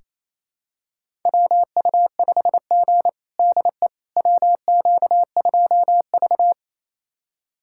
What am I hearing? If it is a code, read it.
Morse code, heard as a keyed tone, about 28 wpm, 700 Hz. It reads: WU5G DE WQ2V
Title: USA Callsign DE Callsign - 4 Characters - 28wpm